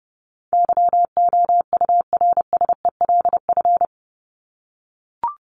Morse code, heard as a keyed tone, about 30 wpm, 700 Hz. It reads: YOURSELF
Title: Top 501-750 Words 30wpm